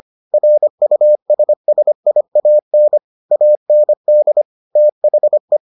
Morse code, heard as a keyed tone, about 25 wpm, 600 Hz. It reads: RUSSIAN AND THE